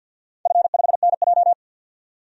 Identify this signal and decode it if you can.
Morse code, heard as a keyed tone, about 50 wpm, 700 Hz. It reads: F5NJ
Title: DXCC-100 Call Signs - 4 Characters - 50wpm